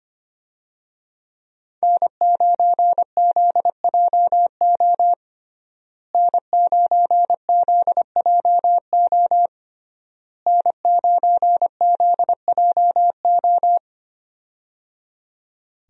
Morse code, heard as a keyed tone, about 25 wpm, 700 Hz. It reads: N9ZJO N9ZJO N9ZJO
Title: USA Call Signs - 5 Characters - Repeated 3x - 25wpm